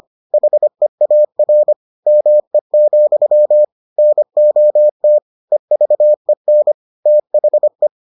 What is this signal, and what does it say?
Morse code, heard as a keyed tone, about 25 wpm, 600 Hz. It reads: HEAR ME, NOT EVEN THE